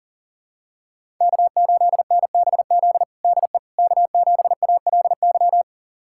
Morse code, heard as a keyed tone, about 40 wpm, 700 Hz. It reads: K8NBZ DE K7ALY
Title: USA Callsign DE Callsign - 5 Characters - 40wpm